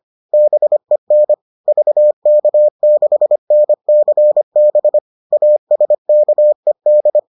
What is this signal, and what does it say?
Morse code, heard as a keyed tone, about 25 wpm, 600 Hz. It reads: BEN VK6NCB ASKED